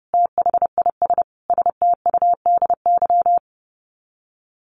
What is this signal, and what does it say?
Morse code, heard as a keyed tone, about 28 wpm, 700 Hz. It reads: THIS STUDY
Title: Sets of 5 Words Encore2 30wpm